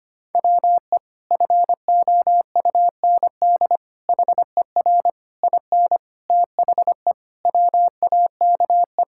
Morse code, heard as a keyed tone, about 25 wpm, 700 Hz. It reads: WE FOUND HER IN THE WAKE
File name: practice / 25wpm / Sentences with 4-Word Phrases - 25wpm